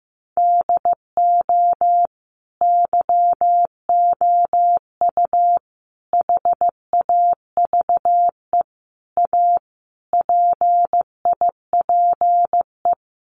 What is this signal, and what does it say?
Morse code, heard as a keyed tone, about 15 wpm, 700 Hz. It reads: DO YOU HAVE A PIPE
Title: Sentences from Top 1500 Words 15wpm